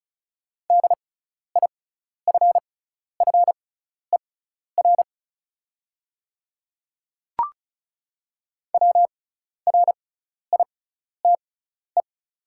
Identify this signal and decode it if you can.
Morse code, heard as a keyed tone, about 36 wpm, spaced as for 12 wpm, 700 Hz. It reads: DIFFER WRITE
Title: 6x Character Spacing - Top 200 Words - 35wpm